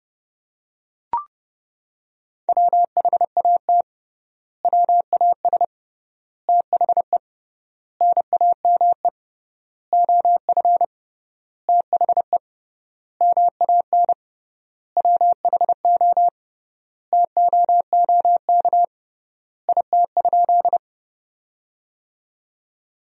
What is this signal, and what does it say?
Morse code, heard as a keyed tone, about 30 wpm, 700 Hz. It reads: WHAT WAS THE NAME OF THE MAN WHO TOOK IT?
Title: 3x Word-Spacing - Sentences from Top 300 Words - 30wpm